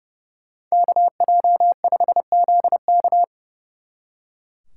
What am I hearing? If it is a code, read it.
Morse code, heard as a keyed tone, about 36 wpm, 700 Hz. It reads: KJ5ZK
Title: Speed Racing - USA 2x2 Call Signs 30wpm